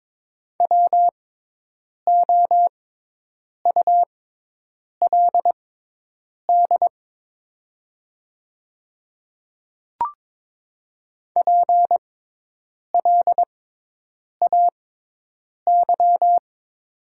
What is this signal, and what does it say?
Morse code, heard as a keyed tone, about 22 wpm, spaced as for 8 wpm, 690 Hz. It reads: WOULD PLAY
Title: 6x Character Spacing - Top 200 Words - 22wpm